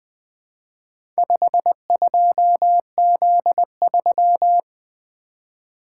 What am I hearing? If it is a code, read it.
Morse code, heard as a keyed tone, about 20 wpm, 700 Hz. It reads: 52Z3
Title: Four Letter-Number Combinations - 20wpm